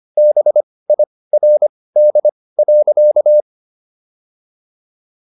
Morse code, heard as a keyed tone, about 25 wpm, spaced as for 18 wpm, 600 Hz. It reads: BIRD.